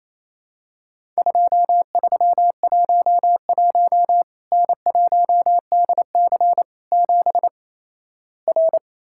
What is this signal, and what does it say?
Morse code, heard as a keyed tone, about 28 wpm, 700 Hz. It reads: 2311 N1DC 7 R
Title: Worked All Europe DX Contest - QTC Exchange - 28wpm